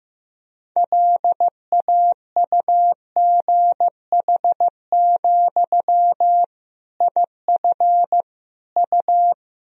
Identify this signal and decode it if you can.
Morse code, heard as a keyed tone, about 15 wpm, 700 Hz. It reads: LAUGH, IF U
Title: Sentences from Top 500 Words and Abbreviations 15wpm